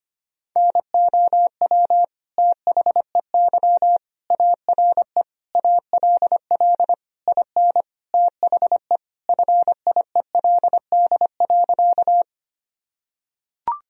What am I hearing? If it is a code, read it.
Morse code, heard as a keyed tone, about 25 wpm, 700 Hz. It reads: NOW THEY ARE ALL IN THE FIELD.
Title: Sentences from Top 1000 Words - Review Part 2 - 25wpm